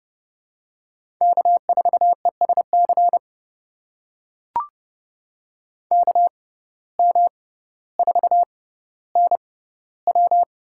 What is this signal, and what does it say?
Morse code, heard as a keyed tone, about 30 wpm, spaced as for 15 wpm, 700 Hz. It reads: K4ESC KM4NW
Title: USA Call Signs - 5 Characters - 6x Character Spacing - 30wpm